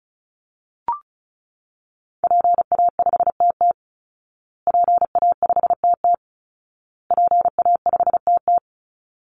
Morse code, heard as a keyed tone, about 34 wpm, 700 Hz. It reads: PA5TT PA5TT PA5TT
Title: Call Signs Repeated 2x - 35wpm